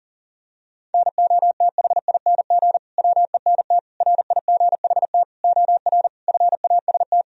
Morse code, heard as a keyed tone, about 40 wpm, 700 Hz. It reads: NOTHING WENT RIGHT OR FAST
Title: Sentences from Top 1000 Words - Review Part 2 - 40wpm